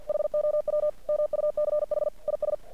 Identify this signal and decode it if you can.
Morse code, heard as a keyed tone, about 50 wpm, 600 Hz. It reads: HOW MUCH IS
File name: archived / CRSnet